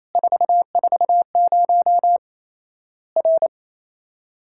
Morse code, heard as a keyed tone, about 28 wpm, 700 Hz. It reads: 440 R
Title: Worked All Europe DX Contest - QTC Exchange - 28wpm